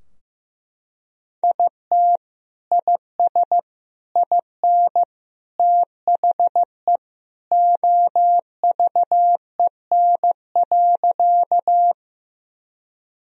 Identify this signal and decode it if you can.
Morse code, heard as a keyed tone, about 15 wpm, 700 Hz. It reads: IT IS IN THE OVEN.
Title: Sentences from Top 1000 Words - Review Part 4 - 15wpm